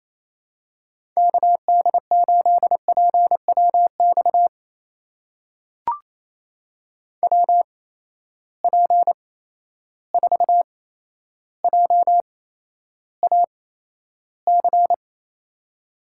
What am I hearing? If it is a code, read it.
Morse code, heard as a keyed tone, about 28 wpm, spaced as for 12 wpm, 700 Hz. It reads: KD8PWX WP4JAC
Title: USA Call Signs - 6 Characters - 8x Character Spacing - 28wpm